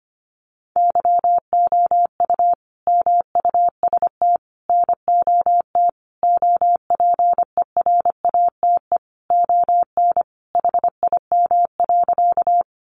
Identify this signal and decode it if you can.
Morse code, heard as a keyed tone, about 25 wpm, 700 Hz. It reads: YOU MUST NOT OPERATE ON HIM.
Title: Sentences from Top 1500 Words 25wpm